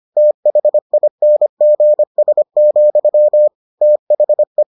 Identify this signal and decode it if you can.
Morse code, heard as a keyed tone, about 25 wpm, 600 Hz. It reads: THINGS, THE